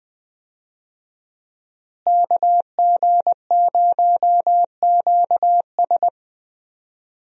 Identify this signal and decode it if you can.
Morse code, heard as a keyed tone, about 20 wpm, 700 Hz. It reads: KG0QS
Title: USA Call Signs - 5 Characters - Repeated 2x - 20wpm